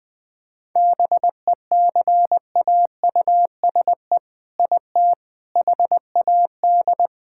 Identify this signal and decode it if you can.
Morse code, heard as a keyed tone, about 20 wpm, 700 Hz. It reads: BECAUSE IT HAD
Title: All-Age Appropriate Jokes - 20wpm